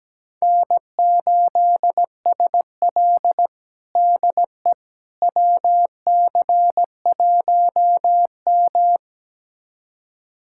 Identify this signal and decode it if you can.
Morse code, heard as a keyed tone, about 17 wpm, 700 Hz. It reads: N8SL DE WC1M
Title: USA Callsign DE Callsign - 4 Characters - 17wpm